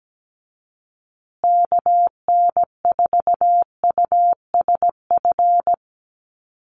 Morse code, heard as a keyed tone, about 17 wpm, 700 Hz. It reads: KN4USF
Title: USA Call Signs Repeated 2x 17wpm